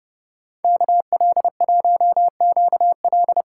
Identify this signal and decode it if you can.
Morse code, heard as a keyed tone, about 36 wpm, 700 Hz. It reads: KL1QL
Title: Speed Racing - USA 2x2 Call Signs 30wpm